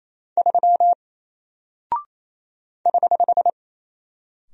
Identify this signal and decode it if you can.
Morse code, heard as a keyed tone, about 28 wpm, 700 Hz. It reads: 3 <HH>
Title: Single Character (Letter:Number:Punctuation:Prosign) Alt 28wpm